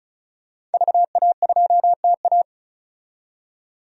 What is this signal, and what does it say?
Morse code, heard as a keyed tone, about 36 wpm, 700 Hz. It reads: VA2TA
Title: DXCC-100 Call Signs - 5 Characters - 35wpm